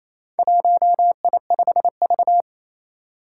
Morse code, heard as a keyed tone, about 27 wpm, 700 Hz. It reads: 1I5V
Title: Four Challenging Letter-Number Combinations 28wpm